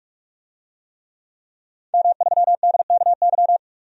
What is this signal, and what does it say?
Morse code, heard as a keyed tone, about 42 wpm, 700 Hz. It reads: M3DKY
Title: DXCC-100 Call Signs - 5 Characters - 45wpm